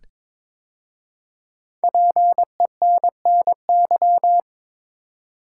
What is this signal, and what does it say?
Morse code, heard as a keyed tone, about 22 wpm, 700 Hz. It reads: PENNY
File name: 5 Letter Words - First 3 Letters Given - 22wpm